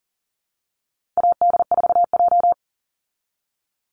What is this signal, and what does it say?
Morse code, heard as a keyed tone, about 40 wpm, 700 Hz. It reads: AD4J
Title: Call Signs - 40wpm